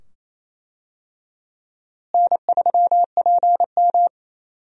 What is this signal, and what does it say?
Morse code, heard as a keyed tone, about 28 wpm, 700 Hz. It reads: N3PM
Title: USA Call Signs - 4 Characters - Repeated 2x - 28wpm